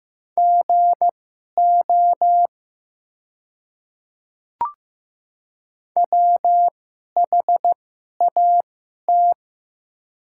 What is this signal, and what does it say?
Morse code, heard as a keyed tone, about 15 wpm, spaced as for 11 wpm, 700 Hz. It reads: GO WHAT
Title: 2x Character Spacing - Top 100 Words - 15wpm